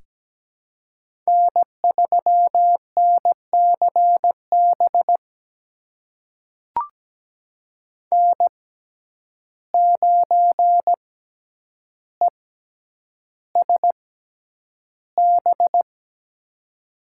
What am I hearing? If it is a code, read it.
Morse code, heard as a keyed tone, about 17 wpm, spaced as for 9 wpm, 700 Hz. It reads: N3NCB N9ESB
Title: USA Call Signs - 5 Characters - 6x Character Spacing - 17wpm